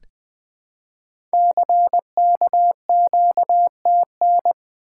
Morse code, heard as a keyed tone, about 20 wpm, 700 Hz. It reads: CKQTN
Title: Code Groups - 5 Letters - 20wpm